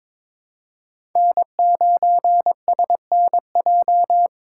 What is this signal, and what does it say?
Morse code, heard as a keyed tone, about 21 wpm, 700 Hz. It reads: N9SNJ
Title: USA Call Signs - 5 Characters - 2x Character Spacing - 22wpm